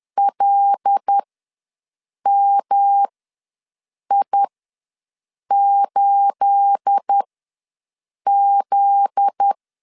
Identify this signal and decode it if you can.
Morse code, heard as a keyed tone, about 10 wpm, spaced as for 6 wpm, 800 Hz. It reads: LMI8Z
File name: LEZIONI / 9 28 ws 10wpm 4124